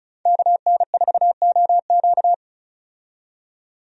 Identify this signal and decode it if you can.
Morse code, heard as a keyed tone, about 36 wpm, 700 Hz. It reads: KN4OQ
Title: USA Call Signs - 5 Characters - 35wpm